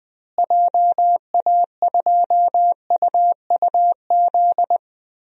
Morse code, heard as a keyed tone, about 21 wpm, 700 Hz. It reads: JA2UUZ
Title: DXCC-100 Call Signs - 6 Characters - Repeated 3x - 20wpm